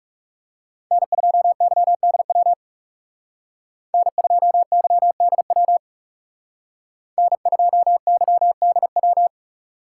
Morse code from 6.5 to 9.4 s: N2YDW

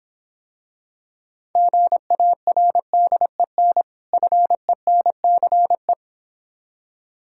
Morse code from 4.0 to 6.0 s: FENCE

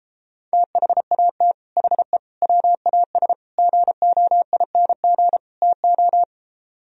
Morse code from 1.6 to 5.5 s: HE WAS GOING

